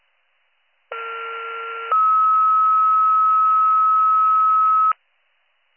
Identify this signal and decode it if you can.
Two-tone page, heard as a one-second tone, then a three-second tone, about 1370 then 1290 Hz. It is heard from 0.9 to 4.9 s.